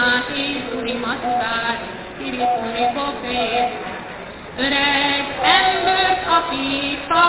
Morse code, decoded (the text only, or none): none